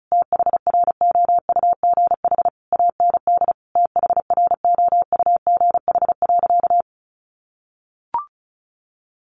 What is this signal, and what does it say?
Morse code, heard as a keyed tone, about 34 wpm, 700 Hz. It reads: THROUGH AND THROUGH.
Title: Sentences from Top 1400 Words 35wpm